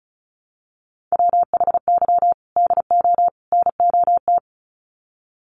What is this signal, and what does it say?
Morse code, heard as a keyed tone, about 34 wpm, 700 Hz. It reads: WHY DO NOT
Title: Sets of 3 Words 35wpm